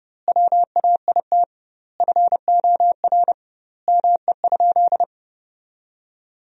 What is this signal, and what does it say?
Morse code, heard as a keyed tone, about 30 wpm, 700 Hz. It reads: WAIT FOR ME?
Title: 2x Word-Spacing - Sentences from Top 500 Words - 30wpm